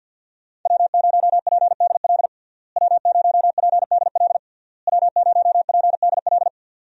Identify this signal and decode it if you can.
Morse code, heard as a keyed tone, about 48 wpm, 700 Hz. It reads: W0PDL W0PDL W0PDL
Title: USA Call Signs - 5 Characters - Repeated 3x - 50wpm